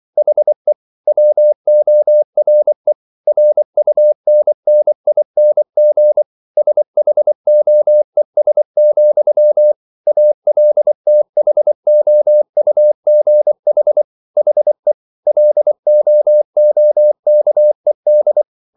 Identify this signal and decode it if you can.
Morse code, heard as a keyed tone, about 24 wpm, 600 Hz. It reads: HE WORE RUNNING SHOES, ALTHOUGH HE LOOKED